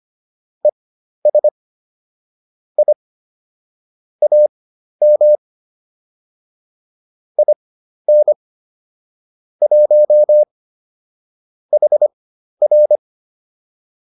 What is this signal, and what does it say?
Morse code, heard as a keyed tone, about 25 wpm, spaced as for 12 wpm, 600 Hz. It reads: ES I AM IN 1 HR